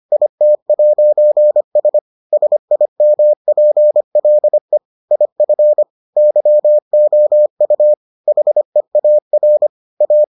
Morse code, heard as a keyed tone, about 25 wpm, 600 Hz. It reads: IT'S SIMPLE IF YOU HEAR A